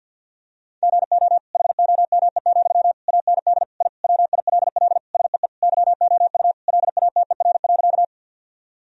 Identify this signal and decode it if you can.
Morse code, heard as a keyed tone, about 50 wpm, 700 Hz. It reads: GO HOME, AND I WILL SEE YOU LATER.
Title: Sentences from Top 1000 Words - Review Part 4 - 50wpm